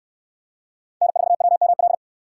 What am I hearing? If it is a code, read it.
Morse code, heard as a keyed tone, about 65 wpm, 690 Hz. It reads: N4WML